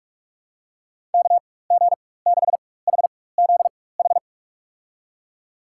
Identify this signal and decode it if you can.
Morse code, heard as a keyed tone, about 43 wpm, spaced as for 21 wpm, 700 Hz. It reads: KG6HZH
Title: USA Call Signs - 6 Characters - 4x Character Spacing - 45wpm